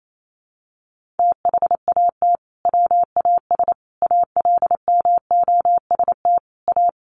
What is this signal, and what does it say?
Morse code, heard as a keyed tone, about 28 wpm, 700 Hz. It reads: THAT WAS ALMOST A
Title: Sentences from Top 1400 Words 28wpm